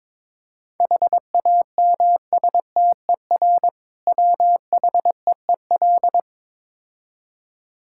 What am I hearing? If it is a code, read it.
Morse code, heard as a keyed tone, about 22 wpm, 690 Hz. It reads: HAMSTER WHEEL
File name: Difficult Words with Hints - 22wpm